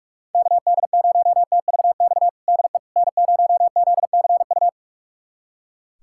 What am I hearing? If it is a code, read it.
Morse code, heard as a keyed tone, about 46 wpm, 700 Hz. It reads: KD0TVX DE N0ZCU
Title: USA Callsign DE Callsign 45wpm